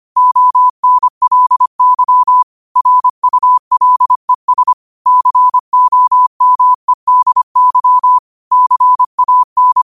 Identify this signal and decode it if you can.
Morse code, heard as a keyed tone, about 25 wpm, 1000 Hz. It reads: ONLY RULES COMEDY CAN